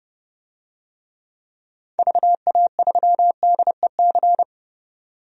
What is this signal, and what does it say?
Morse code, heard as a keyed tone, about 30 wpm, 700 Hz. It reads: VA3DEC
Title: DXCC-100 Call Signs - 6 Characters - 30wpm